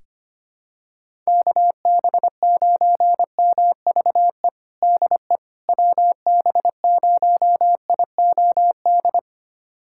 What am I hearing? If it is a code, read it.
Morse code, heard as a keyed tone, about 25 wpm, 700 Hz. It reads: KB9MVE DE WB0IOD
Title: USA Callsign DE Callsign 25wpm